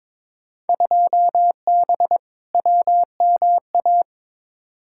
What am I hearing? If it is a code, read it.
Morse code, heard as a keyed tone, about 22 wpm, 700 Hz. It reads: 2B WMA
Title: Field Day Exchanges Encore - 22wpm